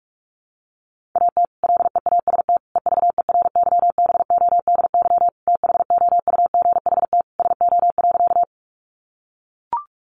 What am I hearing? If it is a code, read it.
Morse code, heard as a keyed tone, about 44 wpm, 700 Hz. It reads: AT LEAST EVERYBODY THOUGHT SO.
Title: Sentences from Top 1400 Words 45wpm